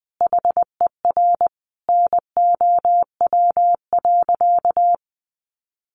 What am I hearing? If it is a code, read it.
Morse code, heard as a keyed tone, about 20 wpm, 700 Hz. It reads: HER NOW.
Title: Sentences from Top 1400 Words 20wpm